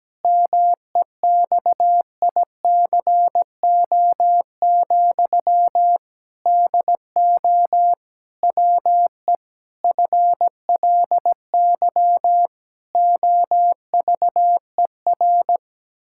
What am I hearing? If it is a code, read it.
Morse code, heard as a keyed tone, about 17 wpm, 700 Hz. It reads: MEXICO, DO WE FLY OVER
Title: Pangram Challenge v2 - 17wpm